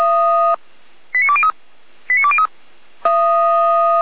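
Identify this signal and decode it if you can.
Five-tone selective call (ZVEI), heard as quick runs of five tones, tones of about 70 ms. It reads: 89203 89203
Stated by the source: Five-tone ZVEI selective call 89305, received on an SDR by fire brigade Feuerwehr Orlamünde.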